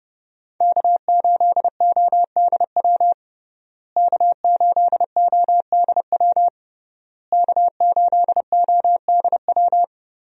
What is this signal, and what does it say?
Morse code, heard as a keyed tone, about 30 wpm, 700 Hz. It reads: K8ODW K8ODW K8ODW
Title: USA Call Signs - 5 Characters - Repeated 3x - 30wpm